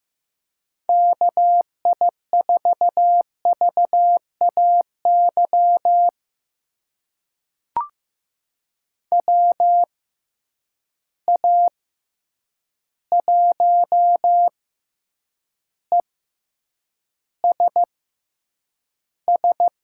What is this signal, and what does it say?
Morse code, heard as a keyed tone, about 15 wpm, spaced as for 8 wpm, 700 Hz. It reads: KI4VAY WA1ESS
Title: USA Call Signs - 6 Characters - 6x Character Spacing - 15wpm